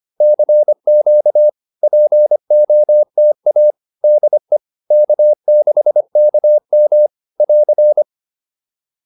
Morse code, heard as a keyed tone, about 25 wpm, 560 Hz. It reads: CQ POTA DE K6KM <AR>